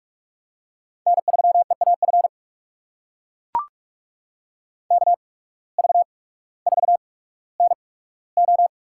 Morse code, heard as a keyed tone, about 44 wpm, spaced as for 19 wpm, 690 Hz. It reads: N3EAF KV4NO
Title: USA Call Signs - 5 Characters - 8x Character Spacing - 45wpm